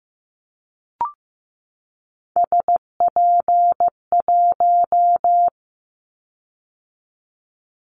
Morse code, heard as a keyed tone, about 15 wpm, 700 Hz. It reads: SP1
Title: Callsign Prefixes for Top 25 DXCC - 15wpm